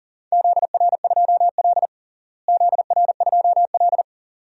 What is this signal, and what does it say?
Morse code, heard as a keyed tone, about 40 wpm, 700 Hz. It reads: ZR2L ZR2L
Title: DXCC-100 Call Signs - 4 Characters - Repeated 2x - 40wpm